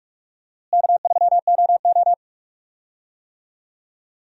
Morse code, heard as a keyed tone, about 41 wpm, 700 Hz. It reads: K3OO
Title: USA Call Signs - 4 Characters - Repeated 3x - 45wpm